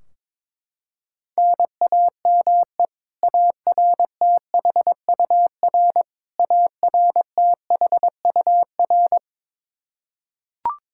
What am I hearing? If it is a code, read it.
Morse code, heard as a keyed tone, about 22 wpm, 700 Hz. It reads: NAME ARTHUR ARTHUR
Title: Practice QSO - Exchange 1 - Beginning - 22wpm